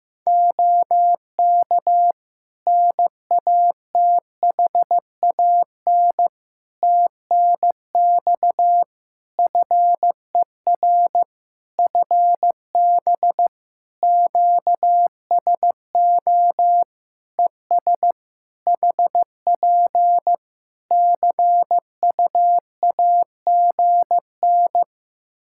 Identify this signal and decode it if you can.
Morse code, heard as a keyed tone, about 15 wpm, 700 Hz. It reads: OK NATHAN TNX FER FB QSO ES HP CUAGN
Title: Practice QSO - Exchange 4 - Ending - 15wpm